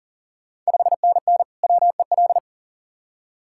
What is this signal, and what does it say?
Morse code, heard as a keyed tone, about 40 wpm, 690 Hz. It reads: 5NN WEL
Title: Ontario QSO Party Exchange 40wpm